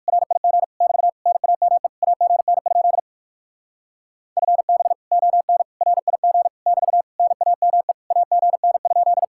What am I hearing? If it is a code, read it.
Morse code, heard as a keyed tone, about 57 wpm, 700 Hz. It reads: RIG <BT> NAME AGN? FB ON RIG <BT> NAME AGN?